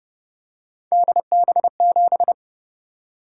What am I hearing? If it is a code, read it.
Morse code, heard as a keyed tone, about 32 wpm, 700 Hz. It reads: DB7